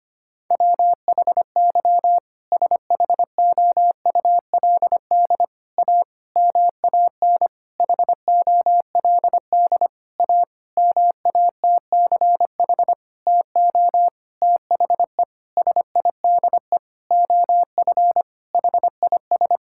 Morse code, heard as a keyed tone, about 25 wpm, 700 Hz. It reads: WHY SHOULD A MAN HOLD A MATCH TO THE SIDE OF HIS
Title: Sentences from Top 1600 Words 25wpm